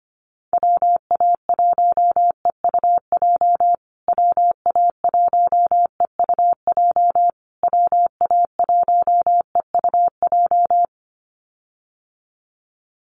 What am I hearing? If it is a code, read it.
Morse code, heard as a keyed tone, about 25 wpm, 700 Hz. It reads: WA1EUJ WA1EUJ WA1EUJ
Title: USA Call Signs Repeated 3x 25wpm